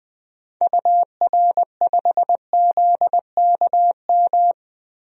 Morse code, heard as a keyed tone, about 20 wpm, 700 Hz. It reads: UR5ZKM
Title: DXCC-100 Call Signs - 6 Characters - Repeated 3x - 20wpm